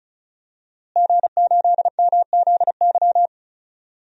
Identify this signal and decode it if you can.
Morse code, heard as a keyed tone, about 36 wpm, 700 Hz. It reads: G8MZY
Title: DXCC-100 Call Signs - 5 Characters - Repeated 2x - 35wpm